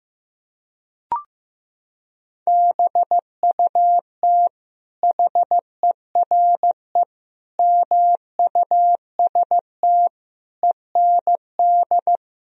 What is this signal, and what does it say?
Morse code, heard as a keyed tone, about 16 wpm, 700 Hz. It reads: BUT HERE MUST END
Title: Sentences from Top 1000 Words - Review Part 1 - 15wpm